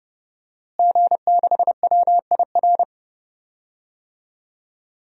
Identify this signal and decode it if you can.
Morse code, heard as a keyed tone, about 32 wpm, 700 Hz. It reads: G6WIR
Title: DXCC-100 Call Signs - 5 Characters - Repeated 3x - 30wpm